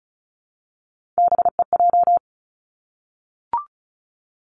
Morse code, heard as a keyed tone, about 36 wpm, 700 Hz. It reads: BEJ
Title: Three Challenging Letter-Number Combinations 35wpm